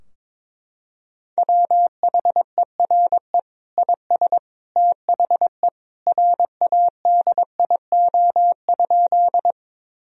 Morse code, heard as a keyed tone, about 22 wpm, 700 Hz. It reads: WHERE IS THE RADIO?
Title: Sentences from Top 1000 Words - Review Part 3 - 22wpm